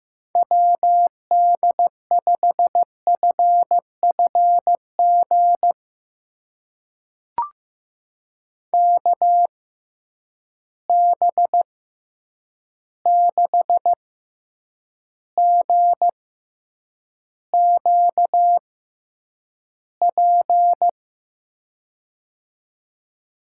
Morse code, heard as a keyed tone, about 15 wpm, spaced as for 8 wpm, 700 Hz. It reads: WD5FFG KB6GQP
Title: USA Call Signs - 6 Characters - 6x Character Spacing - 15wpm